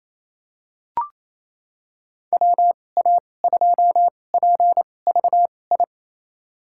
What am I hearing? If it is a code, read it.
Morse code, heard as a keyed tone, about 28 wpm, spaced as for 20 wpm, 700 Hz. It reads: WA2PVI
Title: USA Call Signs - 6 Characters - 2x Character Spacing - 28wpm